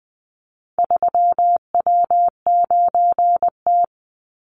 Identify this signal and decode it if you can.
Morse code, heard as a keyed tone, about 19 wpm, 700 Hz. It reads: 3W9T
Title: Call Signs - 20wpm